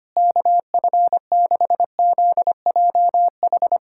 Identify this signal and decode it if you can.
Morse code, heard as a keyed tone, about 25 wpm, 700 Hz. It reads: KF6ZJH
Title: USA Callsign DE Callsign 25wpm